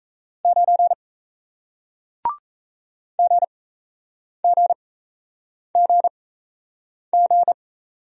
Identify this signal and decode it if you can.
Morse code, heard as a keyed tone, about 38 wpm, spaced as for 8 wpm, 700 Hz. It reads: 9 GGGG